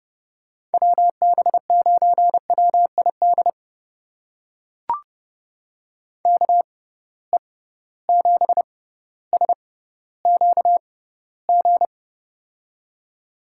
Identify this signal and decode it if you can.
Morse code, heard as a keyed tone, about 30 wpm, spaced as for 15 wpm, 700 Hz. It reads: WB9WID KE7SQG
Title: USA Call Signs - 6 Characters - 6x Character Spacing - 30wpm